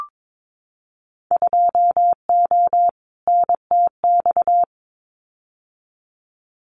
Morse code, heard as a keyed tone, about 22 wpm, 700 Hz. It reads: 2O NTX
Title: Winter Field Day Exchanges Encore 22wpm